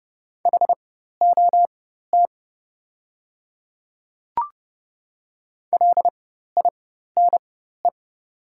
Morse code, heard as a keyed tone, about 30 wpm, spaced as for 14 wpm, 700 Hz. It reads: HOT LINE
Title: 4x Character Spacing - Top 200 Words - 30wpm